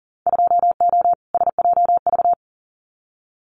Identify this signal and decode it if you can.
Morse code, heard as a keyed tone, about 40 wpm, 700 Hz. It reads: 2O SJV
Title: Winter Field Day Exchanges Encore 40wpm